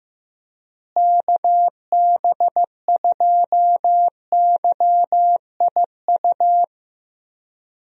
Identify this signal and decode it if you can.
Morse code, heard as a keyed tone, about 15 wpm, 700 Hz. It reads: KB2YIU
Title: USA Call Signs - 6 Characters - Repeated 3x - 15wpm